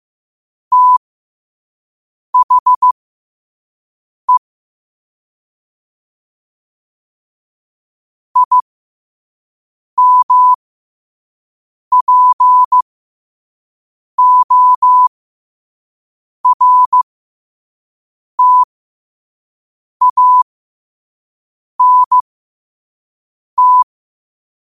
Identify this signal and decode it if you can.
Morse code, heard as a keyed tone, about 15 wpm, spaced as for 5 wpm, 1000 Hz. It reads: THE IMPORTANT
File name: qotdfiles / 5wpm